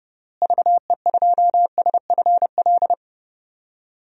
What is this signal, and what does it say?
Morse code, heard as a keyed tone, about 30 wpm, 700 Hz. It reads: VE2SFL
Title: DXCC-100 Call Signs - 6 Characters - Repeated 3x - 30wpm